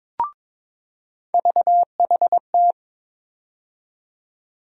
Morse code, heard as a keyed tone, about 22 wpm, 700 Hz. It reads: VHT
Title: USA Call Sign Postfixes - 3 Letters 22wpm